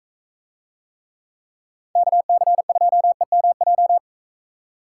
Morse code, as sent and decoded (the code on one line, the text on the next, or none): -.- -.-. ..--- . -- .---
KC2EMJ